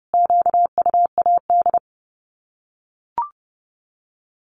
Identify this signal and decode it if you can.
Morse code, heard as a keyed tone, about 30 wpm, 700 Hz. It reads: QUAD
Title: Common QSO Phrases 30wpm